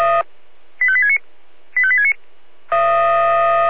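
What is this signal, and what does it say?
Five-tone selective call (ZVEI), heard as quick runs of five tones, tones of about 70 ms. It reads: 85869 85869
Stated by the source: ZVEI five-tone selective call received by SDR.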